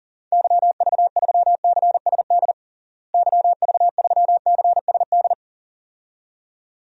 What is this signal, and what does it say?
Morse code, heard as a keyed tone, about 40 wpm, 700 Hz. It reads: YV3CSD YV3CSD
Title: DXCC-100 Call Signs - 6 Characters - Repeated 2x - 40wpm